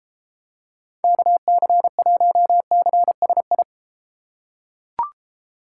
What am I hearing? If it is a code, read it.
Morse code, heard as a keyed tone, about 33 wpm, 700 Hz. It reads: KC1CSI